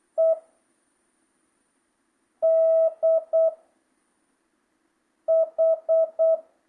Morse code, heard as a keyed tone, about 8 wpm, 660 Hz. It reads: E D H
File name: jh8cbh / 2025cw